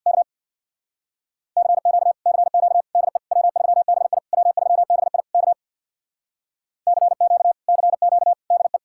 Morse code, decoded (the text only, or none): K CQ CQ DE W3BI W3BI K CQ CQ DE